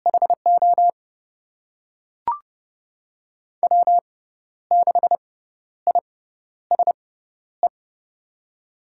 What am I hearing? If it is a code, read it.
Morse code, heard as a keyed tone, about 30 wpm, spaced as for 12 wpm, 700 Hz. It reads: HO W6ISE
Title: USA Call Signs - 6x Character Spacing - 30wpm